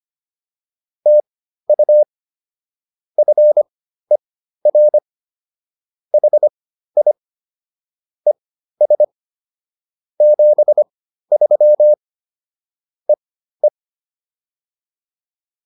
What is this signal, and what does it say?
Morse code, heard as a keyed tone, about 25 wpm, spaced as for 13 wpm, 600 Hz. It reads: TU FER HI ES 73 EE